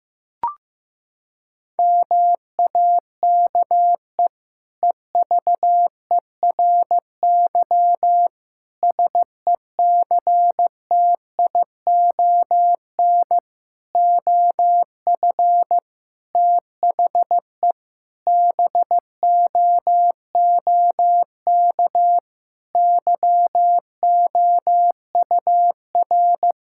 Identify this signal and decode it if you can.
Morse code, heard as a keyed tone, about 15 wpm, 690 Hz. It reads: MAKE EVERY SECTION OF THE BOOK YOUR
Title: Sentences from Top 1000 Words - Review Part 5 - 15wpm